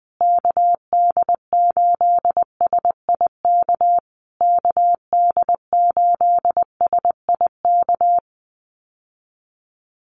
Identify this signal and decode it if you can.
Morse code, heard as a keyed tone, about 20 wpm, 700 Hz. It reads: KD8SIK KD8SIK
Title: USA Call Signs Repeated 3x 20wpm